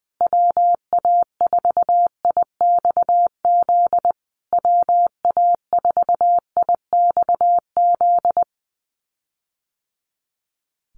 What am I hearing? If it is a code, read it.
Morse code, heard as a keyed tone, about 20 wpm, 700 Hz. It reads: WA4IXZ WA4IXZ
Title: USA Call Signs Repeated 3x 20wpm